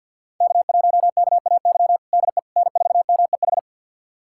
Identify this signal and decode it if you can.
Morse code, heard as a keyed tone, about 47 wpm, 700 Hz. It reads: K1KAY DE N4MEH
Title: USA Callsign DE Callsign - 5 Characters - 50wpm